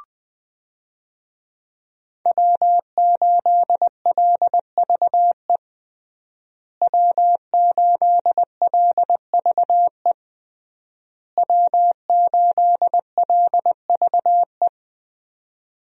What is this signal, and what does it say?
Morse code, heard as a keyed tone, about 20 wpm, 700 Hz. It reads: W8LVE W8LVE W8LVE
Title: USA Call Signs - 5 Characters - Repeated 3x - 20wpm